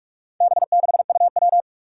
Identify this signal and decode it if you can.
Morse code, heard as a keyed tone, about 44 wpm, 700 Hz. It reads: B6UW
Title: Four Challenging Letter-Number Combinations 45wpm